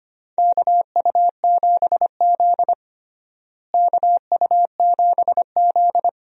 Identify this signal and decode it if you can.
Morse code, heard as a keyed tone, about 25 wpm, 700 Hz. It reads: KU7Z KU7Z
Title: USA Call Signs - 4 Characters - Repeated 3x - 25wpm